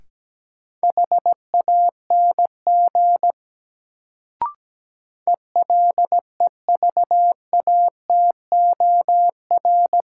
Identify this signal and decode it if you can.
Morse code, heard as a keyed tone, about 17 wpm, 700 Hz. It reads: HANG ELEVATOR
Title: Sentences from Top 1600 Words - New Word Drill Alt - 17wpm